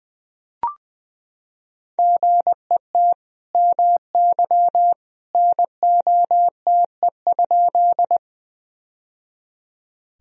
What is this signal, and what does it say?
Morse code, heard as a keyed tone, about 20 wpm, 700 Hz. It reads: GET MY NOTE?
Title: Sentences from Top 1000 Words - Review Part 2 - 20wpm